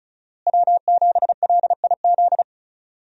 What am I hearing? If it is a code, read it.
Morse code, heard as a keyed tone, about 34 wpm, 700 Hz. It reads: W7LIZ
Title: USA Call Signs - 5 Characters - Repeated 2x - 35wpm